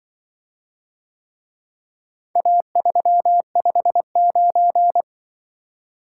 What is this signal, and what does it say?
Morse code, heard as a keyed tone, about 24 wpm, 700 Hz. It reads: A359
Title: Speed-Racing - Four Letter-Number Combinations - 20wpm